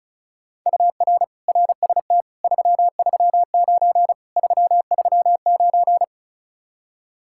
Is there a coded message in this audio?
Yes: Morse code from 0.7 to 6.0 s, heard as a keyed tone, about 44 wpm, 700 Hz. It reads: UR RST 339 339